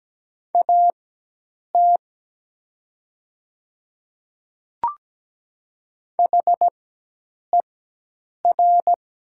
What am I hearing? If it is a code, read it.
Morse code, heard as a keyed tone, about 17 wpm, spaced as for 8 wpm, 700 Hz. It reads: AT HER
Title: 4x Character Spacing - Top 100 Words - 17wpm